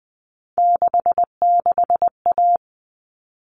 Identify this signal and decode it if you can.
Morse code, heard as a keyed tone, about 20 wpm, 700 Hz. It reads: 66A
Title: All Cut Numbers - 3-Digit 20wpm